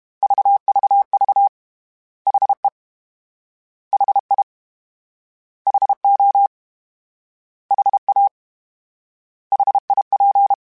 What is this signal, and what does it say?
Morse code, heard as a keyed tone, about 32 wpm, 800 Hz. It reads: VVV HE HI HO HA HIP
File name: H Prefixes / H mp3s